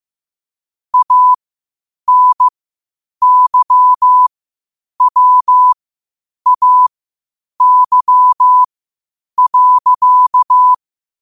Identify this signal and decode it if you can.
Morse code, heard as a keyed tone, about 15 wpm, spaced as for 8 wpm, 1000 Hz. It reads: ANYWAY.
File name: qotdfiles / 8wpm